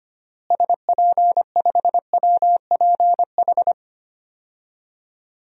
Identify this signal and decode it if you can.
Morse code, heard as a keyed tone, about 25 wpm, 700 Hz. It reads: SP5WPH
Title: DXCC-100 Call Signs - 6 Characters - Repeated 3x - 25wpm